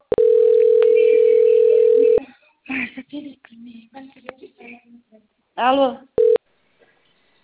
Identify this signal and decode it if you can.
Ringback tone (North America), 440 + 480 Hz, with speech.
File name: out- - - - - .51326